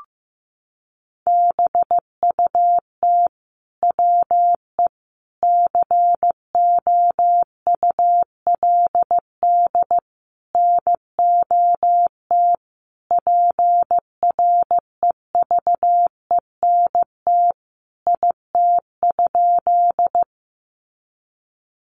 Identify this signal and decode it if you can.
Morse code, heard as a keyed tone, about 15 wpm, 700 Hz. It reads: BUT WE COULD NOT PREVENT IT?
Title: Sentences from Top 1100 Words 15wpm